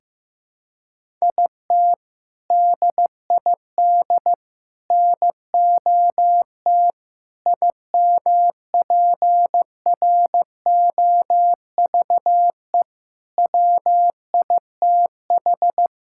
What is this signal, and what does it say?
Morse code, heard as a keyed tone, about 15 wpm, 700 Hz. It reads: IT DID NOT IMPROVE WITH
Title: Sentences from Top 1000 Words - Review Part 5 - 15wpm